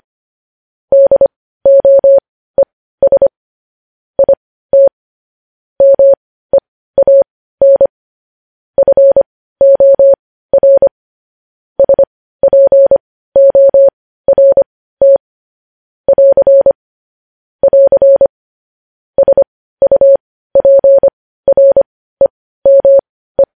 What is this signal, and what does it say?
Morse code, heard as a keyed tone, about 25 wpm, spaced as for 15 wpm, 550 Hz. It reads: DOES IT MEAN FOR SPORT <AR> <AR> SUPREME
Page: news headlines in morse code at 15 wpm